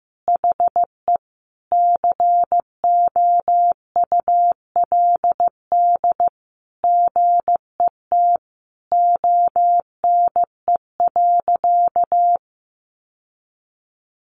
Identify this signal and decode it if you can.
Morse code, heard as a keyed tone, about 15 wpm, 700 Hz. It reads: HE COULD GET ONE.
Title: Sentences from Top 1400 Words 15wpm